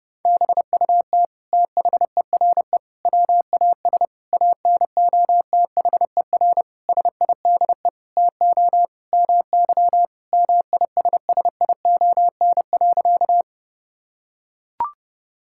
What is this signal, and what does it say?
Morse code, heard as a keyed tone, about 30 wpm, 700 Hz. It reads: BUT THERE WAS ANOTHER SIDE TO MY MISSION.
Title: Sentences from Top 1600 Words 30wpm